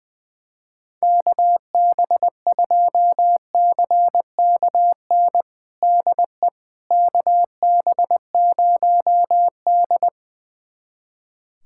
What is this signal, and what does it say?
Morse code, heard as a keyed tone, about 20 wpm, 700 Hz. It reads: KB2CKN DE KB0D
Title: USA Callsign DE Callsign 20wpm